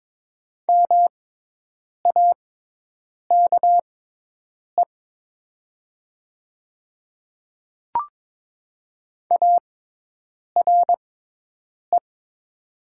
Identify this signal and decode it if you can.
Morse code, heard as a keyed tone, about 22 wpm, spaced as for 8 wpm, 700 Hz. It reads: MAKE ARE
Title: 6x Character Spacing - Top 100 Words - 22wpm